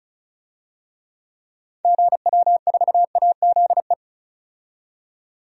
Morse code, heard as a keyed tone, about 36 wpm, 700 Hz. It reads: GW4AZE
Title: DXCC-100 Call Signs - 6 Characters - 35wpm